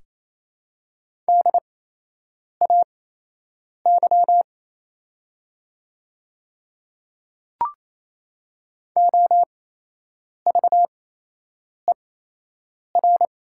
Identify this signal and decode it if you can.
Morse code, heard as a keyed tone, about 28 wpm, spaced as for 8 wpm, 690 Hz. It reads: DAY OVER
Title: 8x Character Spacing - Top 100 Words - 28wpm